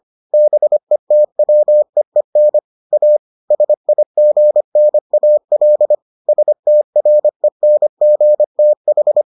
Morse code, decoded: BETWEEN A SIGNAL STRENGTH

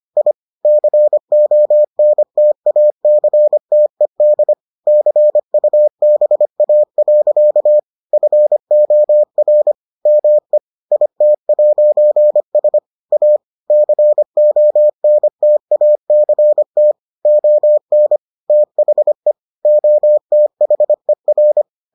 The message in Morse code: I CONTACTED CUBA. FOR ME IT'S A CONTACT ON THE OTHER

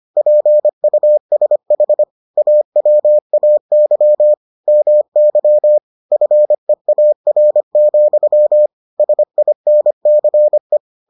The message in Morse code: PUSH AWAY MY FEAR, SINCE

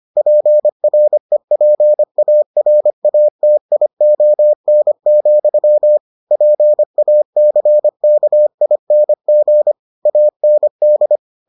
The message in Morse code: PREPARATION, PACKING AND

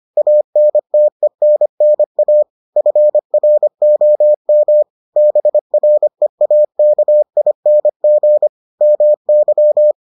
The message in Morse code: ANTENNA FROM BREAKING MY

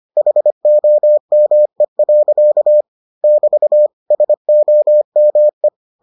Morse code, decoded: HOME. <BT> SOME

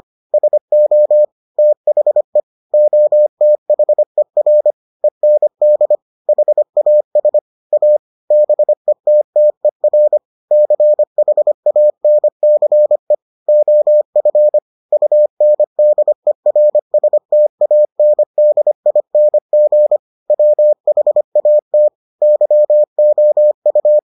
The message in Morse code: SO THE OTHER END HAS A BETTER CHANCE OF UNDERSTANDING WHAT YOU

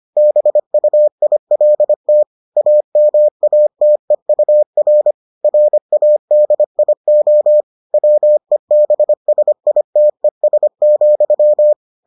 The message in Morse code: BUILT AMATEUR RADIO WEBSITES,

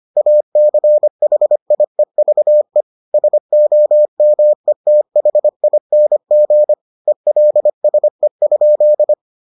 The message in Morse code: ACHIEVE SOMETHING ELSE?

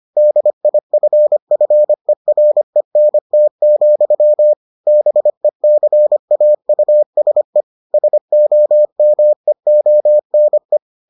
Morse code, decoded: DIFFERENT, BECAUSE SOMEONE

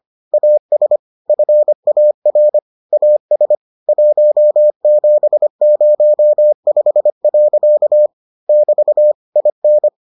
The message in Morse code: AS FAR AS 1705. <BT> IN

